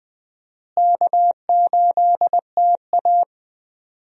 Morse code, heard as a keyed tone, about 21 wpm, 700 Hz. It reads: K8TA
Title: USA Call Signs - 4 Characters - Repeated 2x - 20wpm